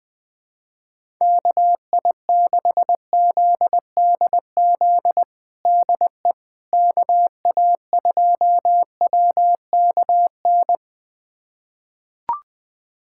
Morse code, heard as a keyed tone, about 21 wpm, 700 Hz. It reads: KI6ZDZ DE KA2WKN
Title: USA Callsign DE Callsign 20wpm